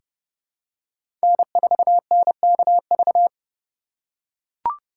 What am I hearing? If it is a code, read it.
Morse code, heard as a keyed tone, about 30 wpm, 700 Hz. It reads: N4NKV
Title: USA Call Signs - 4x Character Spacing - 30wpm